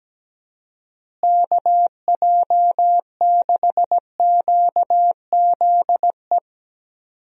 Morse code, heard as a keyed tone, about 17 wpm, 700 Hz. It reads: KJ6QZE
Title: USA Call Signs - 6 Characters - Repeated 2x - 17wpm